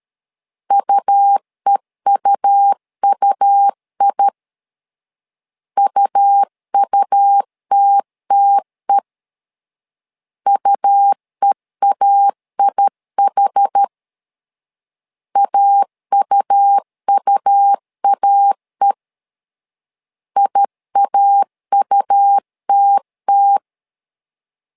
Morse code, heard as a keyed tone, about 13 wpm, 800 Hz. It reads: UEUUI UUTTE UEAIH AUUAE IAUTT
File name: LEZIONI / 2 UHDN / 2 UH 12wpm 1439